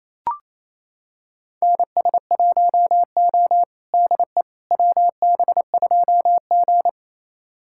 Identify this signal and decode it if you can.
Morse code, heard as a keyed tone, about 28 wpm, 700 Hz. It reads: NS1O DE WB2G
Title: USA Callsign DE Callsign - 4 Characters - 28wpm